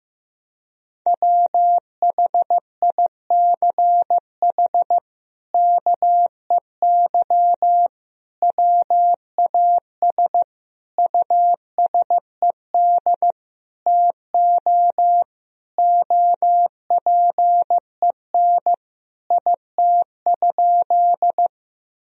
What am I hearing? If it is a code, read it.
Morse code, heard as a keyed tone, about 16 wpm, 700 Hz. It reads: WHICH KEY WAS USED TO OPEN IT?
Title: Sentences from Top 1000 Words - Review Part 4 - 15wpm